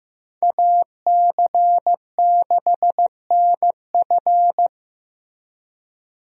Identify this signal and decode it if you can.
Morse code, heard as a keyed tone, about 15 wpm, 700 Hz. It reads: AC6NF
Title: USA Call Signs - 5 Characters - Repeated 2x - 15wpm